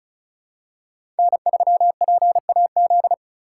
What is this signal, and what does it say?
Morse code, heard as a keyed tone, about 36 wpm, 700 Hz. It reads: N3PAZ
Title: USA Call Signs - 5 Characters - 2x Character Spacing - 35wpm